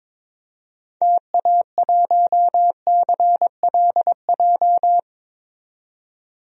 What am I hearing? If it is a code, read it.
Morse code, heard as a keyed tone, about 22 wpm, 700 Hz. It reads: TA1CLJ
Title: DXCC-100 Call Signs - 6 Characters - 22wpm